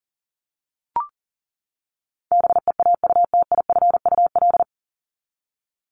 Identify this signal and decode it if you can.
Morse code, heard as a keyed tone, about 40 wpm, 700 Hz. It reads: BEAUTIFUL
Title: Top 751-1000 Words 40wpm